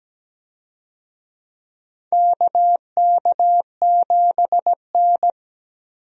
Morse code, heard as a keyed tone, about 17 wpm, 700 Hz. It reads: KK7N
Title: USA Call Signs - 4 Characters - Repeated 3x - 17wpm